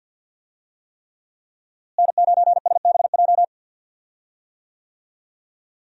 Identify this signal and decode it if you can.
Morse code, heard as a keyed tone, about 48 wpm, 700 Hz. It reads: N9SBJ
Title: USA Call Signs - 5 Characters - Repeated 3x - 50wpm